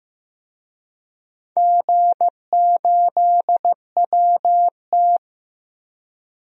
Morse code, heard as a keyed tone, about 15 wpm, 700 Hz. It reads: G8WT
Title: DXCC-100 Call Signs - 4 Characters - Repeated 2x - 15wpm